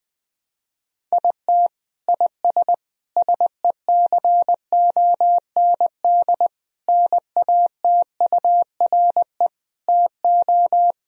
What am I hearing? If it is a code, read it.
Morse code, heard as a keyed tone, about 21 wpm, 700 Hz. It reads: IT IS SECOND NATURE TO
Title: Sentences from Top 1000 Words - Review Part 1 - 20wpm